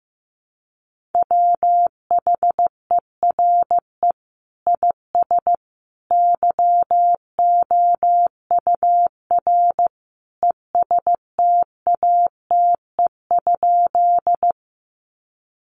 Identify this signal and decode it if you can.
Morse code, heard as a keyed tone, about 15 wpm, 700 Hz. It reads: WHERE IS YOUR ESTATE?
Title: Sentences from Top 1200 Words 15wpm